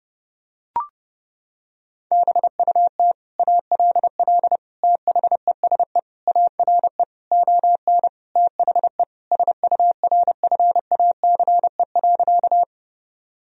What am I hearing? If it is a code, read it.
Morse code, heard as a keyed tone, about 30 wpm, 700 Hz. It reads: BUT ALL THESE ARE ON THE SURFACE.
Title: Sentences from Top 1000 Words - Review Part 3 - 30wpm